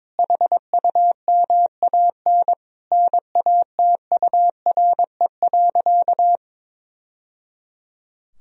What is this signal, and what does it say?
Morse code, heard as a keyed tone, about 22 wpm, 700 Hz. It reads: HUMAN NATURE.
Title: Sentences from Top 1000 Words - Review Part 3 - 22wpm